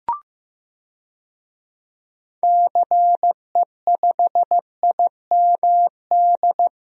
Morse code, heard as a keyed tone, about 15 wpm, 700 Hz. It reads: CE5IMD
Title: DXCC-100 Call Signs - 6 Characters - Repeated 2x - 15wpm